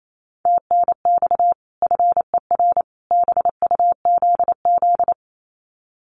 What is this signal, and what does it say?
Morse code, heard as a keyed tone, about 28 wpm, 700 Hz. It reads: TNX FER BUZZ
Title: Common QSO Phrases 28wpm